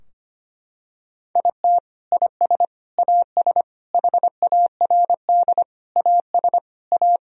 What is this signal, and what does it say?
Morse code, heard as a keyed tone, about 25 wpm, 700 Hz. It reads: IT IS AS HARD AS A
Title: Sentences from Top 1000 Words - Review Part 2 - 25wpm